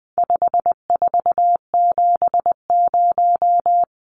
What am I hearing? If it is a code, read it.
Morse code, heard as a keyed tone, about 20 wpm, 700 Hz. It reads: 5470
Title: Numbers - 4-Digit 20wpm